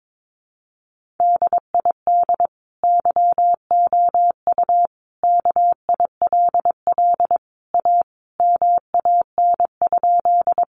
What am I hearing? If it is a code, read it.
Morse code, heard as a keyed tone, about 22 wpm, 700 Hz. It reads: DID YOU KILL A MAN?
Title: Sentences from Top 1400 Words 22wpm